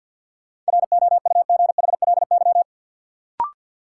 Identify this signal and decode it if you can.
Morse code, heard as a keyed tone, about 52 wpm, 700 Hz. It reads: ROUGHLY
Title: Sentences from Top 1500 Words - New Word Drill Alt - 50wpm